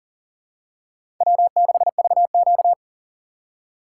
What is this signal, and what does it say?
Morse code, heard as a keyed tone, about 40 wpm, 700 Hz. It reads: W6VQ
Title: USA Call Signs - 4 Characters - 2x Character Spacing - 40wpm